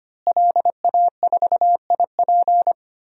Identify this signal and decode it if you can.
Morse code, heard as a keyed tone, about 25 wpm, 700 Hz. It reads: LA4IP
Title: DXCC-100 Call Signs - 5 Characters - Repeated 2x - 25wpm